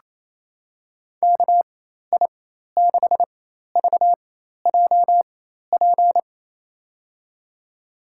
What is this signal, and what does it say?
Morse code, heard as a keyed tone, about 27 wpm, spaced as for 13 wpm, 700 Hz. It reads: KI6VJP
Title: USA Call Signs - 6 Characters - 4x Character Spacing - 28wpm